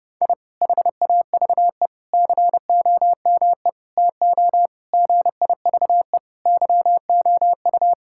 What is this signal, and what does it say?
Morse code, heard as a keyed tone, about 30 wpm, 700 Hz. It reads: I HAVE COME TO GIVE YOU
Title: Sentences from Top 1000 Words - Review Part 6 - 30wpm